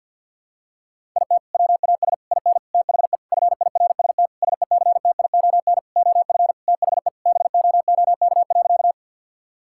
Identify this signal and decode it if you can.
Morse code, heard as a keyed tone, about 49 wpm, 700 Hz. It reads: IT WAS IN THE FIRST SECTION OF THE BOOK.
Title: Sentences from Top 1000 Words - Review Part 5 - 50wpm